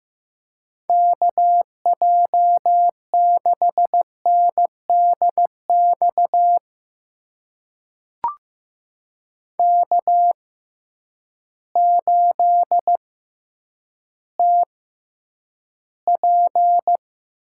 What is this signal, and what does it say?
Morse code, heard as a keyed tone, about 15 wpm, spaced as for 9 wpm, 700 Hz. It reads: KJ6NDX K8TP
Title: USA Call Signs - 6x Character Spacing - 15wpm